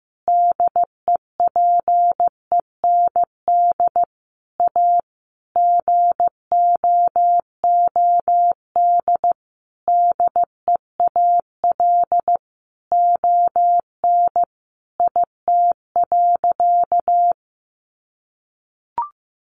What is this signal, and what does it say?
Morse code, heard as a keyed tone, about 15 wpm, 700 Hz. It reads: DEPEND A GOOD DEAL ON IT.
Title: Sentences from Top 1300 Words 15wpm